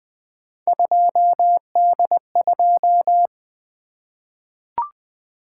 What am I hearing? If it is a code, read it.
Morse code, heard as a keyed tone, about 20 wpm, 700 Hz. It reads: 2D2
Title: Three Letter-Number Combinations - 20wpm